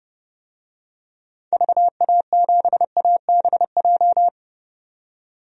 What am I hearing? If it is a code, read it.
Morse code, heard as a keyed tone, about 30 wpm, 700 Hz. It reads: VA7ABJ
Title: DXCC-100 Call Signs - 6 Characters - 30wpm